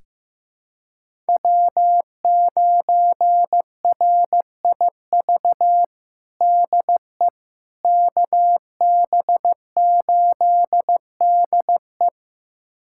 Morse code, heard as a keyed tone, about 15 wpm, 700 Hz. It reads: W9RIV DE KB8DE
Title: USA Callsign DE Callsign - 5 Characters - 15wpm